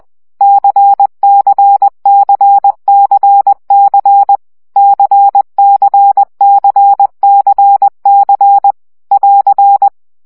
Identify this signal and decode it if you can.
Morse code, heard as a keyed tone, about 20 wpm, 800 Hz. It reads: CCCCC CCCCC <AR>
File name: CW gyakorlas c 17 baud